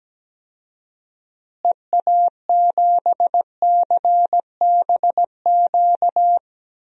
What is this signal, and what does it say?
Morse code, heard as a keyed tone, about 17 wpm, 700 Hz. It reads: EA7CBQ
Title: DXCC-100 Call Signs - 6 Characters - Repeated 3x - 17wpm